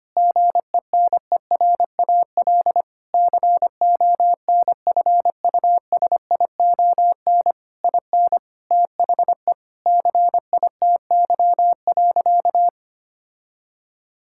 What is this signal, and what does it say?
Morse code, heard as a keyed tone, about 25 wpm, 700 Hz. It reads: GENERAL CONFUSION IN THE CITY.
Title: Sentences from Top 1600 Words 25wpm